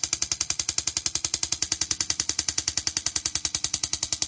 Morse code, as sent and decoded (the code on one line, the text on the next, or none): none